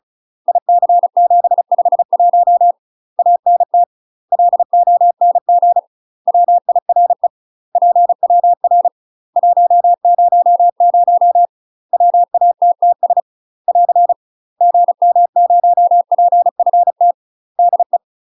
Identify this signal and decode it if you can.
Morse code, heard as a keyed tone, about 34 wpm, 700 Hz. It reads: IC751 ANT LONG WIRE PWR 100 WATTS <AR> GM0PFT DE
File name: UK MASTER QSOs TRAINING / 35WPM / UK MASTER QSOs 0025